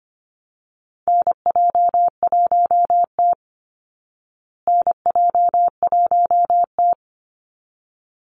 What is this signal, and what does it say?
Morse code, heard as a keyed tone, about 25 wpm, 700 Hz. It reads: NJ1T NJ1T
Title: Call Signs Repeated 3x - 25wpm